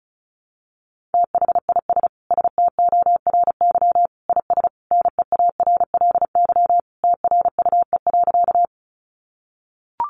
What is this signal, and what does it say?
Morse code, heard as a keyed tone, about 34 wpm, 700 Hz. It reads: THIS STORY IS NEARLY TRUE.
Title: Sentences from Top 1300 Words 35wpm